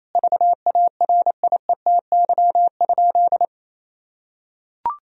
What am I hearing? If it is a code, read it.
Morse code, heard as a keyed tone, about 28 wpm, 700 Hz. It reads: VARIETY?
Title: Sentences from Top 1000 Words - Review Part 4 - 28wpm